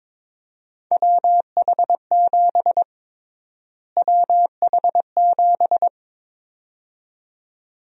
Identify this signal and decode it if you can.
Morse code, heard as a keyed tone, about 22 wpm, 700 Hz. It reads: WH7 WH7
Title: USA Call Sign Prefixes - Repeated 2x - 22wpm